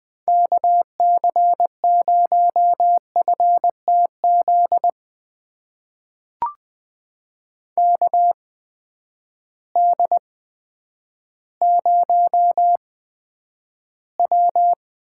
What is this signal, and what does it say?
Morse code, heard as a keyed tone, about 20 wpm, spaced as for 10 wpm, 700 Hz. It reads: KC0FTZ KD0W
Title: USA Call Signs - 6 Characters - 8x Character Spacing - 20wpm